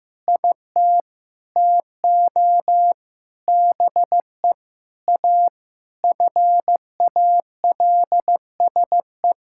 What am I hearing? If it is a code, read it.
Morse code, heard as a keyed tone, about 15 wpm, 700 Hz. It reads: IT TO BE A FALSE
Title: Sentences from Top 1600 Words 15wpm